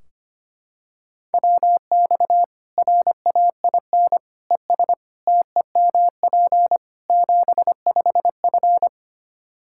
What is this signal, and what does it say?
Morse code, heard as a keyed tone, about 25 wpm, 700 Hz. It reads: WX RAIN ES TEMP 75F
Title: Practice QSO - Exchange 2 - Info - 25wpm